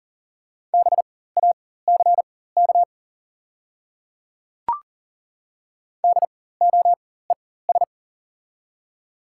Morse code, heard as a keyed tone, about 40 wpm, spaced as for 19 wpm, 700 Hz. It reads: BACK DOES
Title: 4x Character Spacing - Top 200 Words - 40wpm